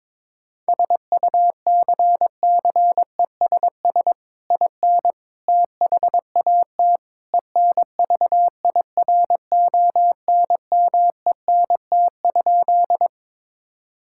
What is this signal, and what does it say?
Morse code, heard as a keyed tone, about 22 wpm, 700 Hz. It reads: SUCCESS IN THAT ENVIRONMENT?
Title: Sentences from Top 1000 Words - Review Part 5 - 22wpm